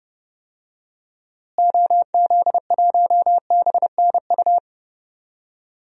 Morse code, heard as a keyed tone, about 30 wpm, 700 Hz. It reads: OZ1BNU
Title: DXCC-100 Call Signs - 6 Characters - Repeated 2x - 30wpm